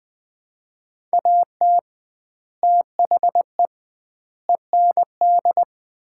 Morse code, heard as a keyed tone, about 20 wpm, 700 Hz. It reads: AT THE END
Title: 2x Word-Spacing - Sets of 5 Words - 20wpm